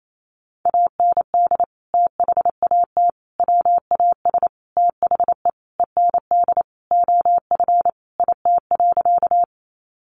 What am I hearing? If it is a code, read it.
Morse code, heard as a keyed tone, about 28 wpm, 700 Hz. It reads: AND THAT WAS THE END OF IT.
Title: Sentences from Top 200 Words 28wpm